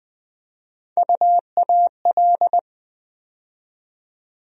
Morse code, heard as a keyed tone, about 20 wpm, 700 Hz. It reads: UAL
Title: USA Call Sign Postfixes - 3 Letters 20wpm